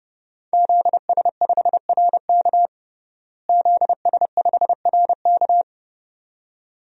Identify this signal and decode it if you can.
Morse code, heard as a keyed tone, about 32 wpm, 700 Hz. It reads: ZS5RK ZS5RK
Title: DXCC-100 Call Signs - 5 Characters - Repeated 2x - 30wpm